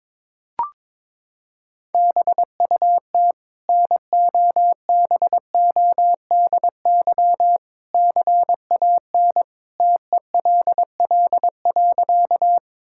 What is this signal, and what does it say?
Morse code, heard as a keyed tone, about 22 wpm, 700 Hz. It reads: BUT NOBODY CAN TELL.
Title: Sentences from Top 1600 Words 22wpm